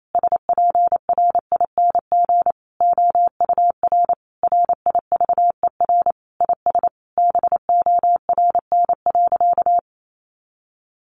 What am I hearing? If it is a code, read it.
Morse code, heard as a keyed tone, about 28 wpm, 700 Hz. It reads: SPRING OUR RIVER IS BORN.
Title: Sentences from Top 1200 Words 28wpm